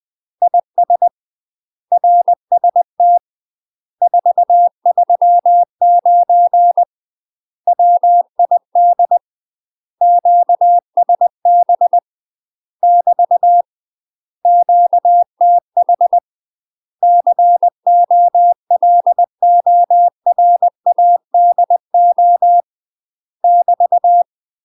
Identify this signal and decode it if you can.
Morse code, heard as a keyed tone, about 20 wpm, 700 Hz. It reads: IS RST 439 WID QSB <BT> QTH COLORADO <BT>
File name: UK MASTER QSOs TRAINING / 20WPM / UK MASTER QSOs 0013